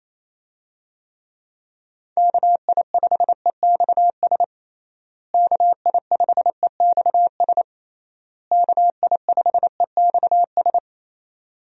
Morse code, heard as a keyed tone, about 28 wpm, 700 Hz. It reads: KI5EXS KI5EXS KI5EXS
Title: USA Call Signs - 6 Characters - Repeated 3x - 28wpm